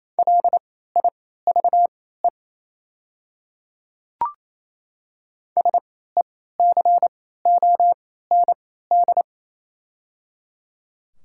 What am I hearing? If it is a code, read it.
Morse code, heard as a keyed tone, about 28 wpm, spaced as for 16 wpm, 700 Hz. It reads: LIVE SECOND
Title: 3x Character Spacing - Top 300 Words - 28wpm